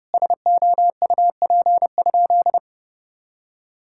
Morse code, heard as a keyed tone, about 30 wpm, 700 Hz. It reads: SOUP?
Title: Sentences from Top 1000 Words - Review Part 6 - 30wpm